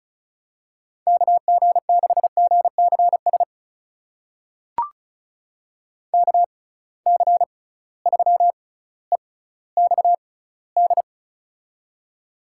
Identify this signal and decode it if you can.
Morse code, heard as a keyed tone, about 36 wpm, spaced as for 18 wpm, 700 Hz. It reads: KG6GCS KC3EXD
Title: USA Call Signs - 6 Characters - 6x Character Spacing - 35wpm